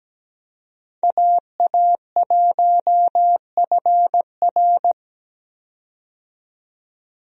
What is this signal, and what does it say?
Morse code, heard as a keyed tone, about 17 wpm, 700 Hz. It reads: AA1FR
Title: USA Call Signs - 5 Characters - 17wpm